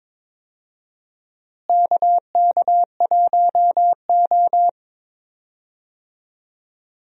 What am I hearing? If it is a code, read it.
Morse code, heard as a keyed tone, about 22 wpm, 700 Hz. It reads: KK1O
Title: USA Call Signs - 4 Characters - Repeated 3x - 22wpm